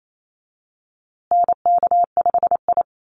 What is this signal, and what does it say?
Morse code, heard as a keyed tone, about 28 wpm, 700 Hz. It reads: NK5I
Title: USA Call Signs Repeated 2x 28wpm